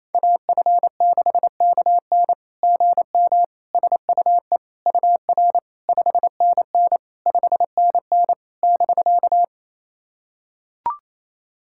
Morse code, heard as a keyed tone, about 28 wpm, 700 Hz. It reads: AF6KN GM SUE UR 5NN 5NN <BK>
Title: SOTA QSO Practice 28wpm